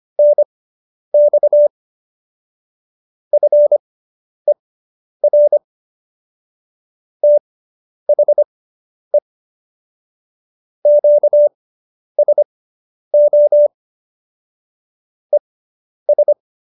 Morse code, heard as a keyed tone, about 25 wpm, spaced as for 10 wpm, 590 Hz. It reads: NX FER THE QSO ES
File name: QSO220 10